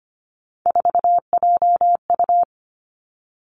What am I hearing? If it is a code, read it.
Morse code, heard as a keyed tone, about 25 wpm, 700 Hz. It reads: 4JU
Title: Three Challenging Letter-Number Combinations 25wpm